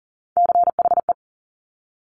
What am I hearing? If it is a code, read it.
Morse code, heard as a keyed tone, about 39 wpm, 700 Hz. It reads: CHE
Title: Top Three Letter Combinations in Words 40wpm